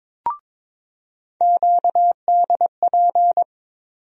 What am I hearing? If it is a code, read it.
Morse code, heard as a keyed tone, about 22 wpm, 700 Hz. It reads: QDP
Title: USA Call Sign Postfixes - 3 Letters 22wpm